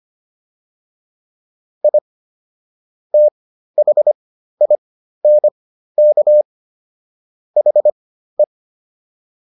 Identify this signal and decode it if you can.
Morse code, heard as a keyed tone, about 25 wpm, spaced as for 13 wpm, 600 Hz. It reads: I THINK HE